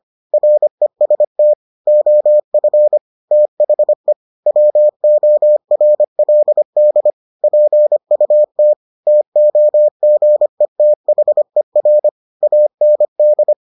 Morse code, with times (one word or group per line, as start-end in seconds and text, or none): 0.3-1.5 REST
1.9-3.0 OF
3.3-4.1 THE
4.5-7.1 WORLD
7.4-8.7 PUT
9.1-12.1 TOGETHER
12.4-13.5 AND